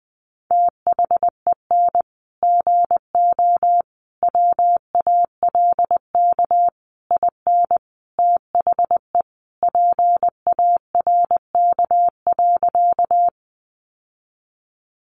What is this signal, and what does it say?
Morse code, heard as a keyed tone, about 19 wpm, 700 Hz. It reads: THEN GO WALK IN THE PARK.
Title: Sentences from Top 1300 Words 20wpm